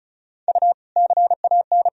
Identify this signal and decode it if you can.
Morse code, heard as a keyed tone, about 36 wpm, 700 Hz. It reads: U CAN
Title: Sentences from Top 200 Words and Abbreviations 35wpm